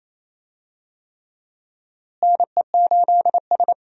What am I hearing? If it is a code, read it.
Morse code, heard as a keyed tone, about 27 wpm, 700 Hz. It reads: NE8S
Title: USA Call Signs - 4 Characters - Repeated 3x - 28wpm